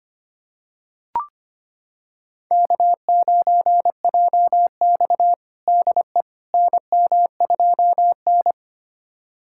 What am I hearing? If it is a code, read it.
Morse code, heard as a keyed tone, about 25 wpm, 700 Hz. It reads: K9JX DE NM2N
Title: USA Callsign DE Callsign - 4 Characters - 25wpm